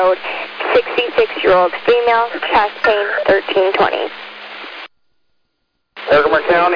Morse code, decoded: none

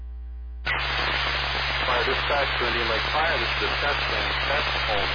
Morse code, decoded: none

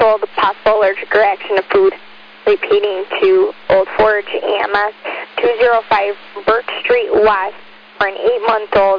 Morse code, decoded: none

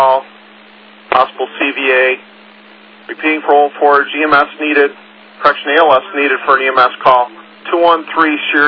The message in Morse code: none

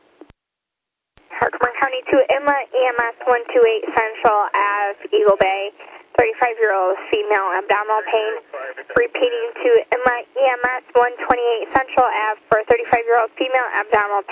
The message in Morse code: none

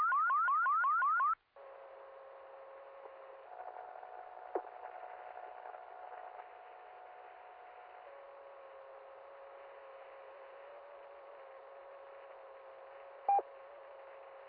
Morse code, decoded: none

